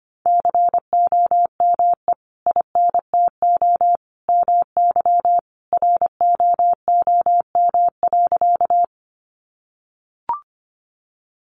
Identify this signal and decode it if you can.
Morse code, heard as a keyed tone, about 25 wpm, 700 Hz. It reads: COME INTO MY ROOM.
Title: Sentences from Top 600 Words 25wpm